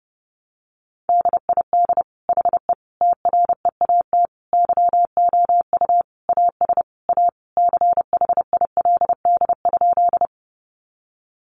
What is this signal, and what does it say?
Morse code, heard as a keyed tone, about 30 wpm, 700 Hz. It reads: DID HE TREAT YOU AS A CHILD?
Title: Sentences from Top 1300 Words 30wpm